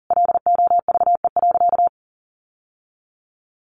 Morse code, heard as a keyed tone, about 40 wpm, 700 Hz. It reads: LOVE.
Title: Sentences from Top 1100 Words 40wpm